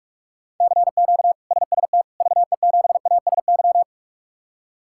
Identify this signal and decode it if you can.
Morse code, heard as a keyed tone, about 42 wpm, 700 Hz. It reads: CQ SST VE7ASY
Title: K1USN SST QSO Practice - 45wpm